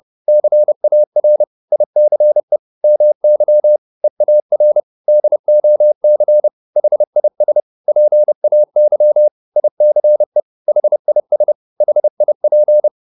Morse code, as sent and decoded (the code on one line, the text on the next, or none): -.-. .- .-. / .. -.-. / -- -.-- / . .- .-. / -.. --- -.-. / .... .. ... / .--. .- -.-- / .. -.-. / .... .. ... / .... .. .--.
CAR ICE MY EAR DOC HIS PAY ICE HIS HIP